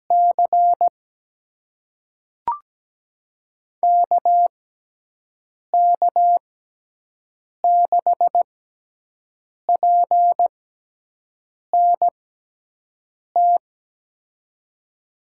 Morse code, heard as a keyed tone, about 17 wpm, spaced as for 6 wpm, 700 Hz. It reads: C KK6PNT
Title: USA Call Signs - 6 Characters - 6x Character Spacing - 17wpm